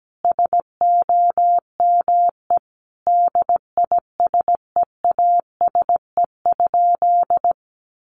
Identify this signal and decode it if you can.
Morse code, heard as a keyed tone, about 17 wpm, 700 Hz. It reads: SOME DISEASE?
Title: Sentences from Top 1500 Words 17wpm